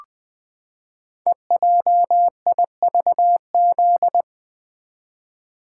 Morse code, heard as a keyed tone, about 20 wpm, 700 Hz. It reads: EJIVZ
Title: Code Groups - 5 Letters - 20wpm